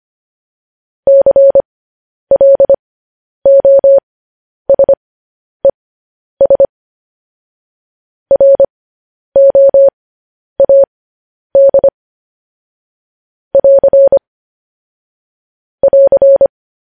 Morse code, heard as a keyed tone, about 25 wpm, spaced as for 10 wpm, 550 Hz. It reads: CLOSES ROAD <AR> <AR>
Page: news headlines in morse code at 10 wpm